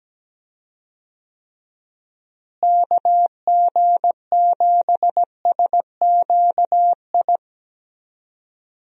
Morse code, heard as a keyed tone, about 17 wpm, 700 Hz. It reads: KG7SQI
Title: USA Call Signs - 6 Characters - Repeated 3x - 17wpm